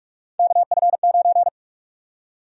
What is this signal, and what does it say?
Morse code, heard as a keyed tone, about 42 wpm, 700 Hz. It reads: KF9
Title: USA Call Sign Prefixes - 45wpm